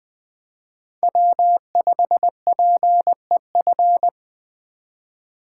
Morse code, heard as a keyed tone, about 24 wpm, 700 Hz. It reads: W5PEF
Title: Speed Racing - USA Call Signs 20wpm